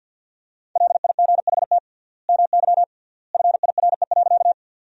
Morse code, heard as a keyed tone, about 50 wpm, 700 Hz. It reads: LIGHT MY FIRE.
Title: 3x Word-Spacing - Sentences from Top 400 Words - 50wpm